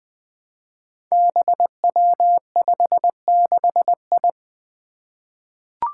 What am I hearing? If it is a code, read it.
Morse code, heard as a keyed tone, about 20 wpm, 700 Hz. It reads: BW56I
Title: Five Challenging Letter-Number Combinations 20wpm